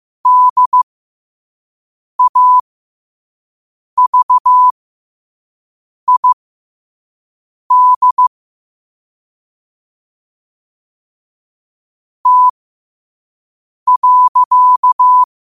Morse code, heard as a keyed tone, about 15 wpm, spaced as for 5 wpm, 1000 Hz. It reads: DAVID T.
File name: qotdfiles / 5wpm